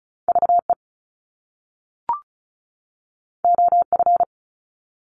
Morse code, heard as a keyed tone, about 36 wpm, 700 Hz. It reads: VE OF
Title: Top Two Letter Combinations in Words 35wpm